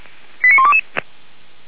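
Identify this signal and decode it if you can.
Five-tone selective call (ZVEI), heard as quick runs of five tones, tones of about 70 ms. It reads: 89122
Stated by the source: ZVEI five-tone call received by SDR.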